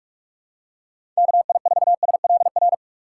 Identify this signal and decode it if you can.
Morse code, heard as a keyed tone, about 44 wpm, 700 Hz. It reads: KI4SLR
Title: USA Call Signs - 6x Character Spacing - 45wpm